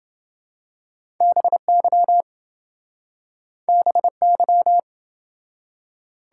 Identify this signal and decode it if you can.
Morse code, heard as a keyed tone, about 26 wpm, 700 Hz. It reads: BY BY